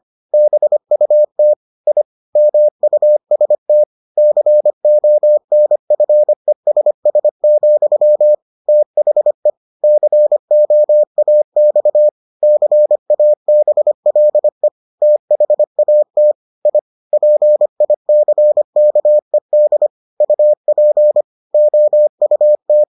Morse code, heard as a keyed tone, about 25 wpm, 600 Hz. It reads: BUT I MUST CONFESS, THE COAX CABLE THAT I PICKED UP OUT